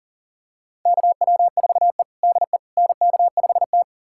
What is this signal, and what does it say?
Morse code, heard as a keyed tone, about 40 wpm, 700 Hz. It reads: KW4E DE NK5T
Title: USA Callsign DE Callsign - 4 Characters - 40wpm